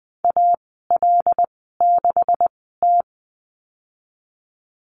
Morse code, heard as a keyed tone, about 20 wpm, spaced as for 15 wpm, 700 Hz. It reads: AL6T
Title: 128 Lesson 21g - Number 6 - Call Signs 20wpm